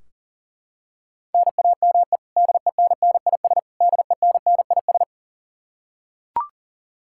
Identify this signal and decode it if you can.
Morse code, heard as a keyed tone, about 40 wpm, 700 Hz. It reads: NAME DENNIS DENNIS
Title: Practice QSO - Exchange 1 - Beginning - 40wpm